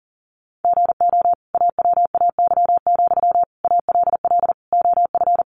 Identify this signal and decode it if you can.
Morse code, heard as a keyed tone, about 40 wpm, 700 Hz. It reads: GO AWAY, ALL OF
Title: Sentences from Top 600 Words 40wpm